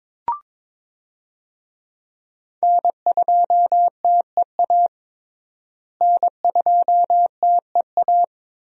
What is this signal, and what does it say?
Morse code, heard as a keyed tone, about 22 wpm, 700 Hz. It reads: N2TEA N2TEA
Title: USA Call Signs - 5 Characters - Repeated 2x - 22wpm